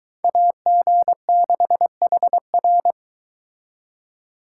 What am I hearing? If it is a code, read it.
Morse code, heard as a keyed tone, about 22 wpm, 700 Hz. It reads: AG6HR